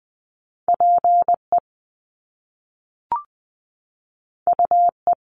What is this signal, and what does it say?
Morse code, heard as a keyed tone, about 20 wpm, 700 Hz. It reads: PE UE
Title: Top Two Letter Combinations in Words 20wpm